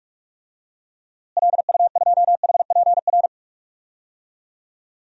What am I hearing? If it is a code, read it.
Morse code, heard as a keyed tone, about 46 wpm, 700 Hz. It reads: LU2HPR
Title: DXCC-100 Call Signs - 6 Characters - Repeated 2x - 45wpm